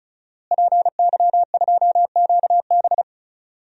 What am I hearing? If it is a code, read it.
Morse code, heard as a keyed tone, about 34 wpm, 700 Hz. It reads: PY2QB
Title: DXCC-100 Call Signs - 5 Characters - Repeated 2x - 35wpm